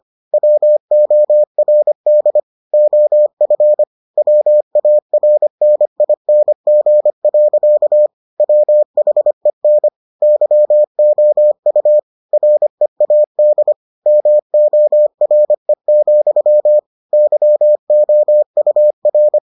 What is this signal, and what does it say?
Morse code, heard as a keyed tone, about 25 wpm, 600 Hz. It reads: WORD OF WARNING. WHEN YOU READ MORE, YOUR